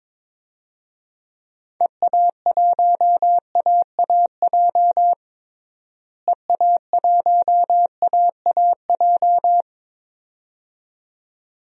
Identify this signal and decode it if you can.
Morse code, heard as a keyed tone, about 21 wpm, 700 Hz. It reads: EA1AAJ EA1AAJ
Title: DXCC-100 Call Signs - 6 Characters - Repeated 2x - 22wpm